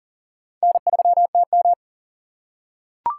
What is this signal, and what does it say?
Morse code, heard as a keyed tone, about 40 wpm, 700 Hz. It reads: N3TM
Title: USA Call Signs - 4 Characters - 6x Character Spacing - 40wpm